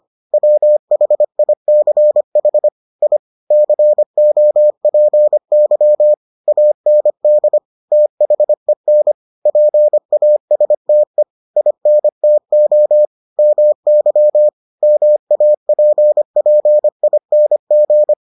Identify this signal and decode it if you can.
Morse code, heard as a keyed tone, about 25 wpm, 600 Hz. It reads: WHICH I COPY AND THEN PASTE INTO MY MAPPING